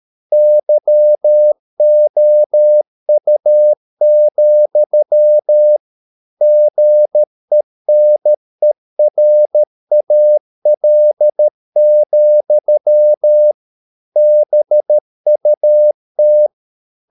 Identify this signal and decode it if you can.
Morse code, heard as a keyed tone, about 13 wpm, 600 Hz. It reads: YOU, GENERAL, BUT